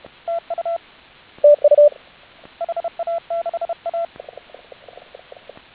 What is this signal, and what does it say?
Morse code, heard as a keyed tone, about 30 wpm, 1350 Hz. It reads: TU HA6A